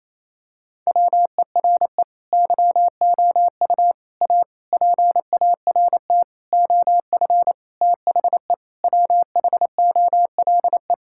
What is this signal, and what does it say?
Morse code, heard as a keyed tone, about 27 wpm, 700 Hz. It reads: WERE YOU A PART OF THE WHOLE
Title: Sentences from Top 1000 Words - Review Part 6 - 28wpm